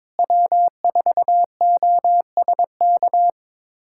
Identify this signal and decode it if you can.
Morse code, heard as a keyed tone, about 22 wpm, 700 Hz. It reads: W4OSK
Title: USA Callsign DE Callsign 22wpm